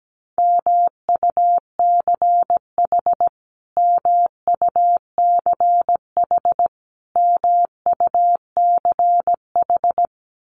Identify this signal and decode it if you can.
Morse code, heard as a keyed tone, about 17 wpm, 700 Hz. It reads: MUCH MUCH MUCH
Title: Top 500 Words Repeated 17wpm